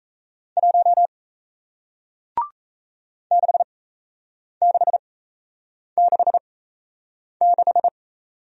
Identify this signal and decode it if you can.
Morse code, heard as a keyed tone, about 38 wpm, spaced as for 8 wpm, 700 Hz. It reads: 1 6666